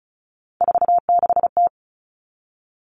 Morse code, heard as a keyed tone, about 34 wpm, 700 Hz. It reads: 46T
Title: Common Cut Numbers - 3-Digit 35wpm